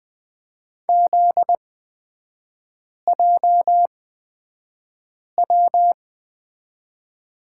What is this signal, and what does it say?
Morse code, heard as a keyed tone, about 20 wpm, spaced as for 5 wpm, 700 Hz. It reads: ZJW
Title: 243 Lesson 39e - Letter z - Three Characters 20wpm